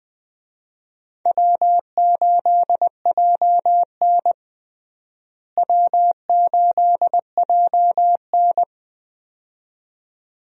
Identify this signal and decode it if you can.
Morse code, heard as a keyed tone, about 20 wpm, 700 Hz. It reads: W8JN W8JN
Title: USA Call Signs - 4 Characters - Repeated 2x - 20wpm